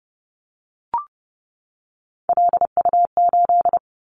Morse code, heard as a keyed tone, about 30 wpm, 700 Hz. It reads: LU8
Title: Callsign Prefixes for Top 50 DXCC - 30wpm